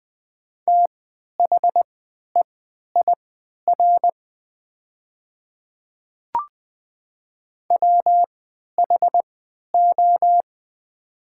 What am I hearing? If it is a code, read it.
Morse code, heard as a keyed tone, about 20 wpm, spaced as for 11 wpm, 700 Hz. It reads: THEIR WHO
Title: 3x Character Spacing - Top 100 Words - 20wpm